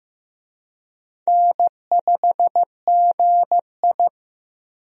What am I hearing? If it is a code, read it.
Morse code, heard as a keyed tone, about 15 wpm, 700 Hz. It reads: N5GI
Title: USA Call Signs - 4 Characters - Repeated 2x - 15wpm